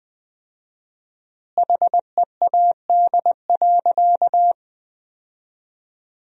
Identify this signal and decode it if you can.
Morse code, heard as a keyed tone, about 21 wpm, 700 Hz. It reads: HEAD.
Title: 4x Word-Spacing - Sentences from Top 300 Words - 20wpm